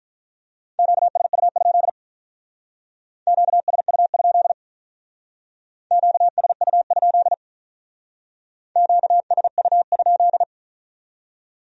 Morse code, heard as a keyed tone, about 46 wpm, 700 Hz. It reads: QSU? QSU? QSU? QSU?